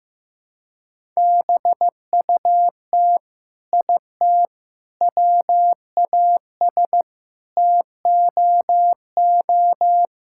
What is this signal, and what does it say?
Morse code, heard as a keyed tone, about 15 wpm, 700 Hz. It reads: BUT IT WAS TOO
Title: Sentences from Top 1600 Words 15wpm